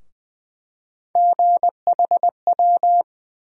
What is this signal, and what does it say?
Morse code, heard as a keyed tone, about 20 wpm, 700 Hz. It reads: GHW
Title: USA Call Sign Postfixes - 3 Letters 20wpm